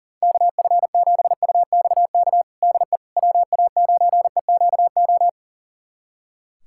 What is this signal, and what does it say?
Morse code, heard as a keyed tone, about 40 wpm, 700 Hz. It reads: KF7UXK DE WA9EQO
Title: USA Callsign DE Callsign 40wpm